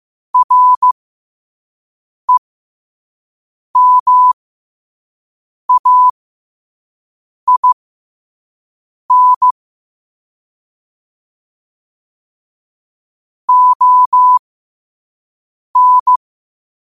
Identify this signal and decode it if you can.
Morse code, heard as a keyed tone, about 15 wpm, spaced as for 5 wpm, 1000 Hz. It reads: REMAIN ON
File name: qotdfiles / 5wpm